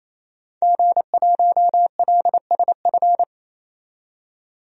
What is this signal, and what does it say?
Morse code, heard as a keyed tone, about 28 wpm, 700 Hz. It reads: G1LSF
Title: DXCC-100 Call Signs - 5 Characters - 28wpm